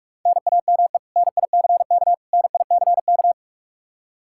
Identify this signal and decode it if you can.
Morse code, heard as a keyed tone, about 41 wpm, 700 Hz. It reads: NAME NICK NICK
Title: Practice QSO - Exchange 1 - Beginning - 45wpm